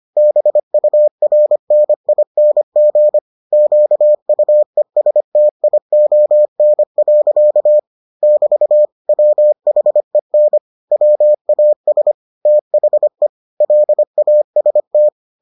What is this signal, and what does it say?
Morse code, heard as a keyed tone, about 25 wpm, 600 Hz. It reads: BURNING QUESTION. <BT> WHEN WAS THE LAST